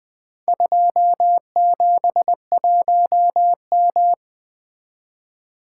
Morse code, heard as a keyed tone, about 20 wpm, 700 Hz. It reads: 271M
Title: Four Letter-Number Combinations - 20wpm